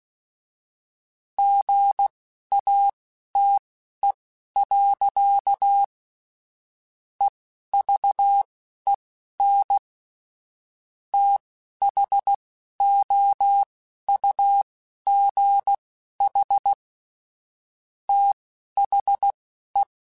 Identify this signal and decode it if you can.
Morse code, heard as a keyed tone, about 16 wpm, spaced as for 11 wpm, 800 Hz. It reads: GATE. EVEN THOUGH THE
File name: Technical Subjects / Technical Subjects-mp3s / AND Gate @10